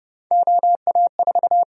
Morse code, heard as a keyed tone, about 29 wpm, 700 Hz. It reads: OA4
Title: USA Call Sign Prefixes - 30wpm